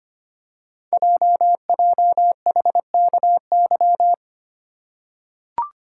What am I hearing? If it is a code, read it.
Morse code, heard as a keyed tone, about 25 wpm, 700 Hz. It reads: JJHKY
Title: Code Groups - 5 Letters - 25wpm